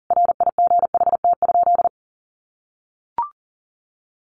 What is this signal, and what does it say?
Morse code, heard as a keyed tone, about 40 wpm, 700 Hz. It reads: RIGHT?
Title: Sentences from Top 800 Words 40wpm